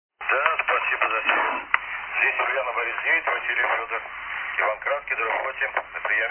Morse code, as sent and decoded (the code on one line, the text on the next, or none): none